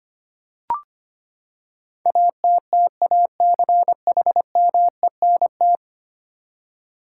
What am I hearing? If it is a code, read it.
Morse code, heard as a keyed tone, about 25 wpm, 700 Hz. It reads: ATTACHMENT
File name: Ten Letter Words - 25wpm